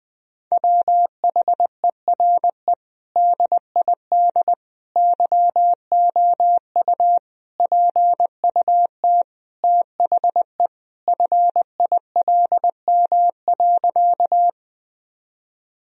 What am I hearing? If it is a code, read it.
Morse code, heard as a keyed tone, about 20 wpm, 700 Hz. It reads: WHERE DID YOU PUT THE FILM.
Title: Sentences from Top 1000 Words - Review Part 3 - 20wpm